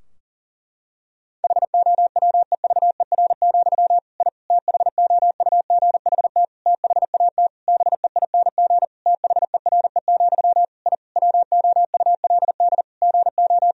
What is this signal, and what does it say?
Morse code, heard as a keyed tone, about 40 wpm, 700 Hz. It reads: HOWEVER, I THOUGHT THAT BEING THERE, I WOULD GO
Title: Sentences from Top 1000 Words - Review Part 3 - 40wpm